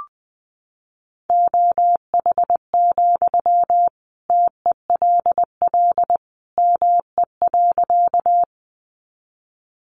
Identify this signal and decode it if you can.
Morse code, heard as a keyed tone, about 20 wpm, 700 Hz. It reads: OH, TELL ME.
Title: Sentences from Top 500 Words 20wpm